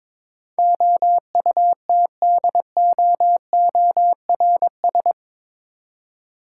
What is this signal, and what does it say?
Morse code, heard as a keyed tone, about 22 wpm, 700 Hz. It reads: OUTDOORS
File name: practice / 22wpm / Eight Letter Words - 22wpm